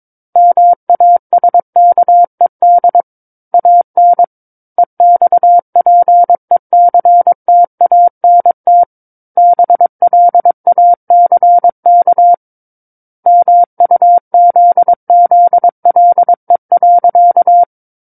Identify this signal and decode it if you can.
Morse code, heard as a keyed tone, about 22 wpm, 700 Hz. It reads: MASKED AN EXPECTANT BLACK MUZZLE.
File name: Book One / 13-Chapter 13 - 22 WPM